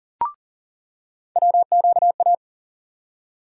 Morse code, heard as a keyed tone, about 40 wpm, 700 Hz. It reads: WQA
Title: USA Call Sign Postfixes - 3 Letters 40wpm